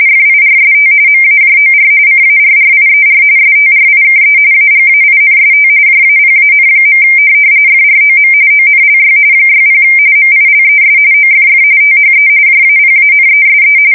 RTTY, about 45.45 baud. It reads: FIRST MAJOR CARRIER STRIKE AGAINST JAPAN. FROM ONLY 60 MILES OFF THE JAPANESE
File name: fsk170 / USS-New-Jersey